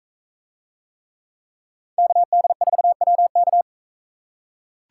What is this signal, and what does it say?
Morse code, heard as a keyed tone, about 40 wpm, 700 Hz. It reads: KD4WK